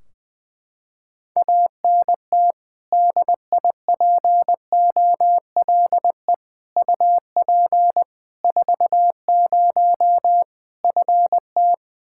Morse code, heard as a keyed tone, about 20 wpm, 700 Hz. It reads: ANT DIPOLE UP 40 FT
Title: Practice QSO - Exchange 2 - Info - 20wpm